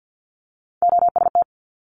Morse code, heard as a keyed tone, about 50 wpm, 700 Hz. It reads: OST
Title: Multi-letter Phonemes Quick Fire 50wpm